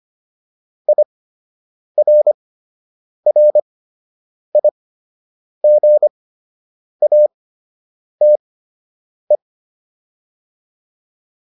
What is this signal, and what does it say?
Morse code, heard as a keyed tone, about 25 wpm, spaced as for 8 wpm, 600 Hz. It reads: IRRIGATE